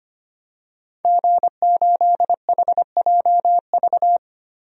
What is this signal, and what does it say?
Morse code, heard as a keyed tone, about 25 wpm, 700 Hz. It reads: G8HJV
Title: DXCC-100 Call Signs - 5 Characters - 25wpm